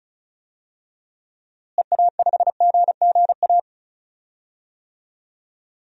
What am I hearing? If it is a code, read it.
Morse code, heard as a keyed tone, about 36 wpm, 700 Hz. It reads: EA5GGA
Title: DXCC-100 Call Signs - 6 Characters - Repeated 3x - 35wpm